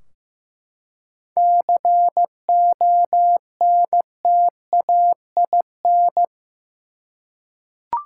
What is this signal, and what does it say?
Morse code, heard as a keyed tone, about 15 wpm, 700 Hz. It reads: CONTAIN
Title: Seven Letter Words - 15wpm